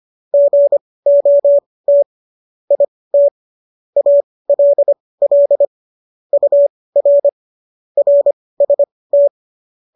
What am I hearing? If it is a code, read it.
Morse code, heard as a keyed tone, about 25 wpm, spaced as for 18 wpm, 570 Hz. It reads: GOT IT ALL UR RST